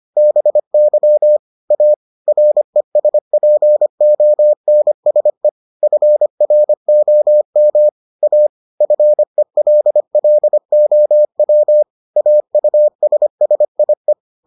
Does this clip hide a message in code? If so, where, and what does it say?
Yes: Morse code from 0.2 to 14.1 s, heard as a keyed tone, about 25 wpm, 600 Hz. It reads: BY A RESPONSE FROM A FELLOW AUSSIE